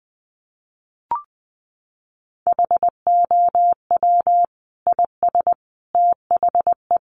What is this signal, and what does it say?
Morse code, heard as a keyed tone, about 19 wpm, 700 Hz. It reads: HOW IS THE
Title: Sentences from Top 800 Words 20wpm